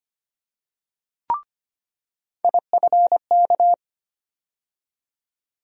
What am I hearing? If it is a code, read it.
Morse code, heard as a keyed tone, about 25 wpm, 700 Hz. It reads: IFK
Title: USA Call Sign Postfixes - 3 Letters 25wpm